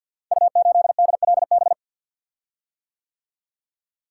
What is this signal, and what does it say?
Morse code, heard as a keyed tone, about 50 wpm, 700 Hz. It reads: U8DLB
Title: Code Groups - 5 Letter:Numbers - 50wpm